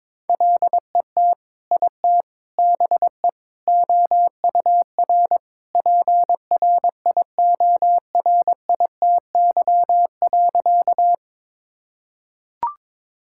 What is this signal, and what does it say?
Morse code, heard as a keyed tone, about 22 wpm, 700 Hz. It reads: LET IT BE OUR PRIORITY.
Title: Sentences from Top 1600 Words 22wpm